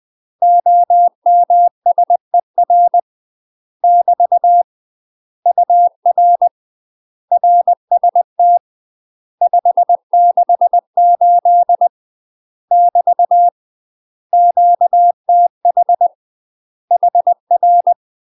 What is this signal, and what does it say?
Morse code, heard as a keyed tone, about 20 wpm, 700 Hz. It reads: OMSER <BT> UR RST 568 <BT> QTH HR
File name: UK MASTER QSOs TRAINING / 20WPM / UK MASTER QSOs 0024